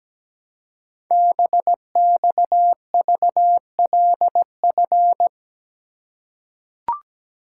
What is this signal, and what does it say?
Morse code, heard as a keyed tone, about 17 wpm, 700 Hz. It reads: BXVLF
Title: Code Groups - 5 Letters - 17wpm